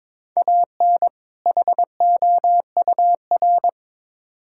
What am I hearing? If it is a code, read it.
Morse code, heard as a keyed tone, about 22 wpm, 700 Hz. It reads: AN HOUR
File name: practice / 22wpm / Build A Phrase - 5-Word Phrases - 22wpm